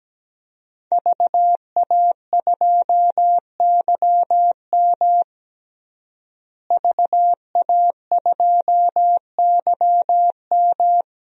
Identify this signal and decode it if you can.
Morse code, heard as a keyed tone, about 17 wpm, 700 Hz. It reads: VA2YM VA2YM
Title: DXCC-100 Call Signs - 5 Characters - Repeated 3x - 17wpm